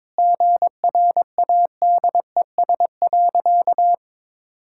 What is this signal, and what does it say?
Morse code, heard as a keyed tone, about 22 wpm, 700 Hz. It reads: GRADES.
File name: practice / 22wpm / Sentences with 4-Word Phrases - 22wpm